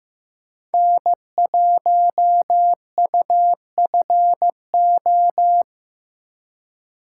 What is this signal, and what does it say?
Morse code, heard as a keyed tone, about 15 wpm, 700 Hz. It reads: N1UFO
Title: USA Call Signs - 3x Character Spacing - 15wpm